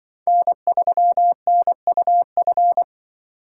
Morse code, heard as a keyed tone, about 24 wpm, 700 Hz. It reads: N3NUF
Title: Speed Racing - USA 1x3 Call Signs 20wpm